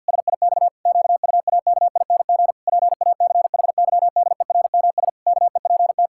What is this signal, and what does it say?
Morse code, heard as a keyed tone, about 52 wpm, 700 Hz. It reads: SIX QUAKING PACHYDERMS KEPT
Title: Pangram Challenge v2 - 50wpm